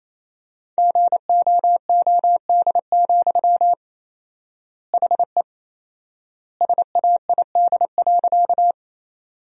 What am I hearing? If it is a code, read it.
Morse code, heard as a keyed tone, about 27 wpm, 700 Hz. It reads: GOOD, HE SAID.
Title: 4x Word-Spacing - Sentences from Top 200 Words - 28wpm